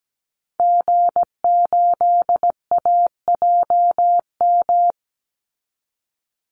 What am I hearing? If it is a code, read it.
Morse code, heard as a keyed tone, about 17 wpm, 700 Hz. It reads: G8AJM
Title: Call Signs Repeated 2x - 17wpm